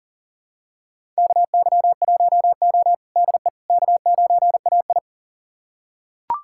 Morse code, heard as a keyed tone, about 40 wpm, 700 Hz. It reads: KY1O DE K9AI
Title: USA Callsign DE Callsign - 4 Characters - 40wpm